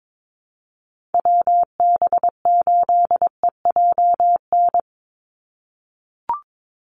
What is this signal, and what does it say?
Morse code, heard as a keyed tone, about 22 wpm, 700 Hz. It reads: WB8EJN
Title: Call Signs - 22wpm